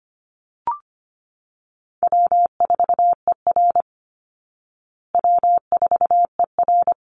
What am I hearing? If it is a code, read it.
Morse code, heard as a keyed tone, about 25 wpm, 700 Hz. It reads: W4ER W4ER
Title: Call Signs Repeated 3x - 25wpm